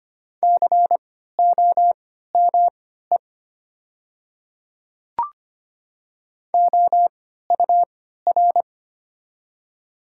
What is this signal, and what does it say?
Morse code, heard as a keyed tone, about 25 wpm, spaced as for 14 wpm, 700 Hz. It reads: COME OUR
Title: 3x Character Spacing - Top 500 Words - 25wpm